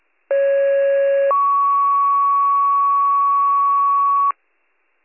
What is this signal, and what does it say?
Two-tone page, heard as a one-second tone, then a three-second tone, about 550 then 1120 Hz. It is heard from 0.3 to 4.3 s.